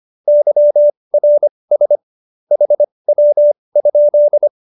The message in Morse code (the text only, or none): YRS HW?